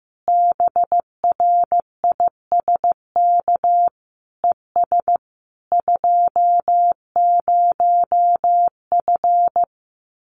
Morse code, heard as a keyed tone, about 15 wpm, 700 Hz. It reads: BRISK ES 20F
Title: Common QSO Phrases Encore 15wpm